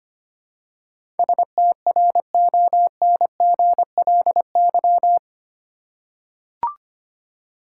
Morse code, heard as a keyed tone, about 25 wpm, 700 Hz. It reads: STRONGLY
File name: 8 Letter Words - First 3 Letters Given - 25wpm